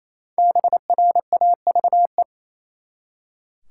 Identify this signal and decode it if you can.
Morse code, heard as a keyed tone, about 28 wpm, 700 Hz. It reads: BRAVE
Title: Five Letter Words - 28wpm